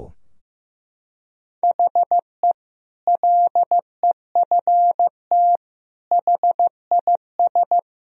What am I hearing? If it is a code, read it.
Morse code, heard as a keyed tone, about 15 wpm, 700 Hz. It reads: HE LEFT HIS
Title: Sentences from Top 1000 Words - Review Part 2 - 15wpm